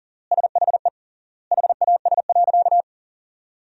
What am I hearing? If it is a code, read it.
Morse code, heard as a keyed tone, about 40 wpm, 700 Hz. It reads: SHE HAS.
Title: 3x Word-Spacing - Sentences from Top 500 Words - 40wpm